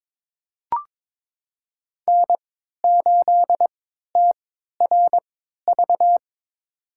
Morse code, heard as a keyed tone, about 22 wpm, spaced as for 13 wpm, 700 Hz. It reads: N8TRV
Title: USA Call Signs - 5 Characters - 3x Character Spacing - 22wpm